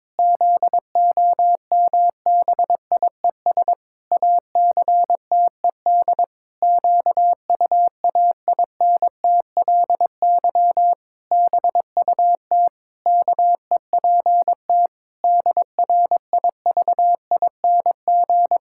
Morse code, heard as a keyed tone, about 22 wpm, 700 Hz. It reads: ZOMBIES ACTED QUAINTLY BUT KEPT DRIVING
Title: Pangram Challenge v2 - 22wpm